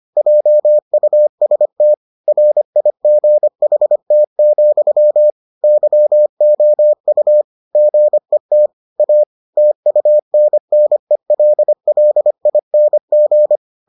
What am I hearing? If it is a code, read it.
Morse code, heard as a keyed tone, about 25 wpm, 600 Hz. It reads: JUST RIGHT, YOU GET A TUNNELLING